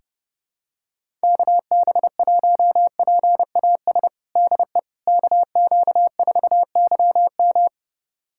Morse code, heard as a keyed tone, about 30 wpm, 700 Hz. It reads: KB1PAS DE KQ4YM
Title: USA Callsign DE Callsign 30wpm